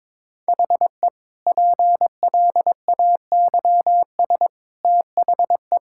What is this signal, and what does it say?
Morse code, heard as a keyed tone, about 22 wpm, 700 Hz. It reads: HE PLAYS THE
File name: practice / 22wpm / Easy Sentences with Hints - 22wpm